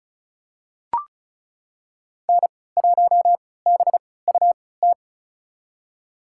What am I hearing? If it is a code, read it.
Morse code, heard as a keyed tone, about 34 wpm, spaced as for 20 wpm, 700 Hz. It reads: N1BUT
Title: USA Call Signs - 3x Character Spacing - 35wpm